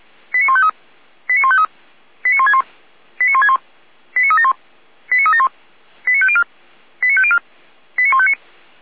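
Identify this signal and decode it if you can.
Five-tone selective call (ZVEI), heard as quick runs of five tones, tones of about 70 ms. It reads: 89153 89153 89161 89161 89371 89371 89504 89504 89159
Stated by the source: Five-tone ZVEI selective call received by SDR, from the audio server of the Feuerwehr Orlamünde fire brigade.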